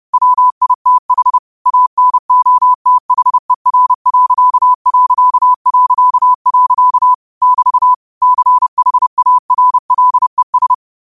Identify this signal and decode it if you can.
Morse code, heard as a keyed tone, about 30 wpm, 1000 Hz. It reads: WITH ANOTHER.... <BT> CHARLES
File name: qotdfiles / 30wpm